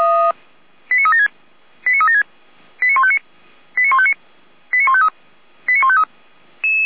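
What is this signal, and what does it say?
Five-tone selective call (ZVEI), heard as quick runs of five tones, tones of about 80 ms. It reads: 89376 89376 89159 89159 89153 89153
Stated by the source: ZVEI five-tone call received by SDR, from the feuerwehr-orlamuende.org server (fire brigade Orlamünde).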